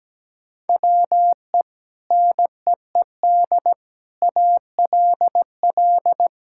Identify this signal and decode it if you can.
Morse code, heard as a keyed tone, about 17 wpm, 700 Hz. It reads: WE NEED ALL
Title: Sentences from Top 1000 Words - Review Part 5 - 17wpm